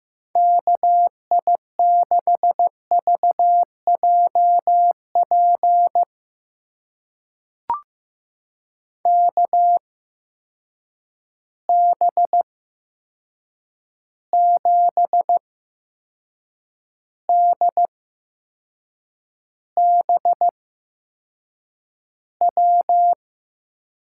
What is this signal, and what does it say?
Morse code, heard as a keyed tone, about 15 wpm, spaced as for 6 wpm, 700 Hz. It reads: KI6VJP KB7DBW
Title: USA Call Signs - 6 Characters - 8x Character Spacing - 15wpm